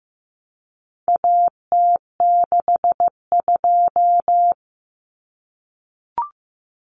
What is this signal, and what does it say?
Morse code, heard as a keyed tone, about 15 wpm, 700 Hz. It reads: AT62
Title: Common Cut Numbers - 4-Digit 15wpm